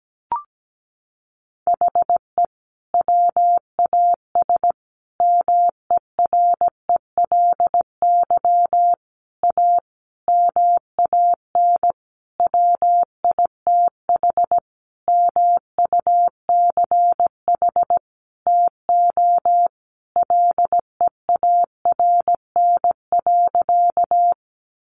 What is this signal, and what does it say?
Morse code, heard as a keyed tone, about 17 wpm, 700 Hz. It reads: HE WAS MERELY A MAN WITH MUCH TO LEARN.
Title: Sentences from Top 1400 Words 17wpm